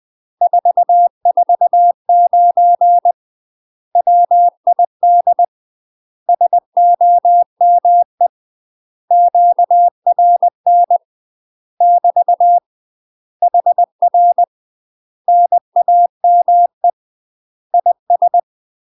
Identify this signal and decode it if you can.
Morse code, heard as a keyed tone, about 20 wpm, 700 Hz. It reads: 449 WID SOME QRN <BT> HR NAME IS
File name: UK MASTER QSOs TRAINING / 20WPM / UK MASTER QSOs 0010